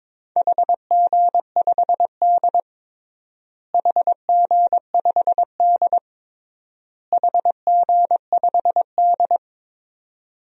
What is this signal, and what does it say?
Morse code, heard as a keyed tone, about 22 wpm, 700 Hz. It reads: HG5D HG5D HG5D
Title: DXCC-100 Call Signs - 4 Characters - Repeated 3x - 22wpm